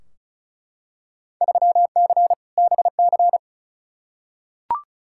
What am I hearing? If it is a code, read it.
Morse code, heard as a keyed tone, about 36 wpm, 700 Hz. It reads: 3C BC
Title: Field Day Exchanges Encore - 35wpm